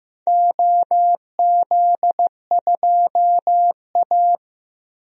Morse code, heard as a keyed tone, about 15 wpm, 700 Hz. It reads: OZ2A
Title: DXCC-100 Call Signs - 4 Characters - Repeated 3x - 15wpm